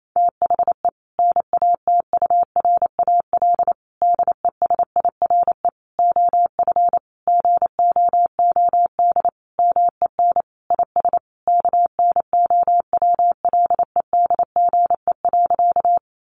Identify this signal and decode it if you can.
Morse code, heard as a keyed tone, about 28 wpm, 700 Hz. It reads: THE NATURAL DESIRE OF GOOD MEN IS KNOWLEDGE.
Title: Sentences from Top 1100 Words 28wpm